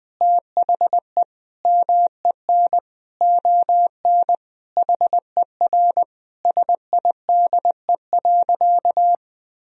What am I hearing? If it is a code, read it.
Morse code, heard as a keyed tone, about 20 wpm, 700 Hz. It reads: THE MEN ON HER SIDE.
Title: Sentences from Top 1000 Words - Review Part 1 - 20wpm